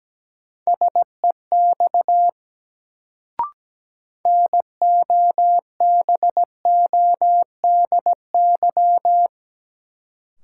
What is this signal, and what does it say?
Morse code, heard as a keyed tone, about 17 wpm, 700 Hz. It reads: SEX NOBODY
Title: Sentences from Top 1600 Words - New Word Drill Alt - 17wpm